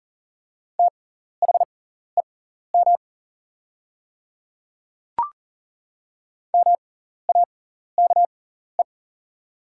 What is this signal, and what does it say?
Morse code, heard as a keyed tone, about 39 wpm, spaced as for 14 wpm, 690 Hz. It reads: THEM MAKE
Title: 6x Character Spacing - Top 100 Words - 40wpm